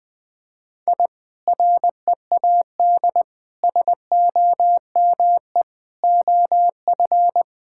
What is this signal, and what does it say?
Morse code, heard as a keyed tone, about 20 wpm, 700 Hz. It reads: I READ SOME OF
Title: Sentences with 2-Word Phrases - 20wpm